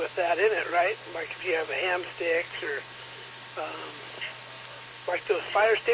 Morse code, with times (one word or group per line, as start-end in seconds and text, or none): none